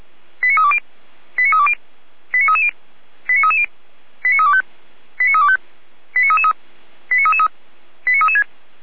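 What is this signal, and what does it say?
Five-tone selective call (ZVEI), heard as quick runs of five tones, tones of about 70 ms. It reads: 89320 89320 89330 89330 89325 89325 89333 89336
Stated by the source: ZVEI five-tone selective call received by SDR.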